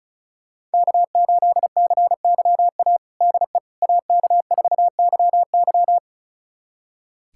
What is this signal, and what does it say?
Morse code, heard as a keyed tone, about 34 wpm, 690 Hz. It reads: K8CYA DE AK4YY
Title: USA Callsign DE Callsign - 5 Characters - 35wpm